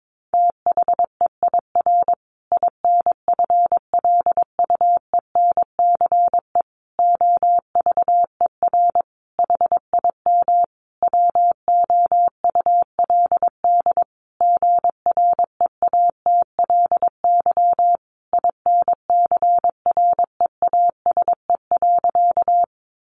Morse code, heard as a keyed tone, about 22 wpm, 700 Hz. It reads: THEIR INFLUENCE OVER HIM WOULD GREATLY INCREASE.
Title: Sentences from Top 1400 Words 22wpm